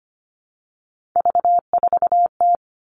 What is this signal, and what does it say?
Morse code, heard as a keyed tone, about 25 wpm, 700 Hz. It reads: V4T
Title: All Cut Numbers - 3-Digit 25wpm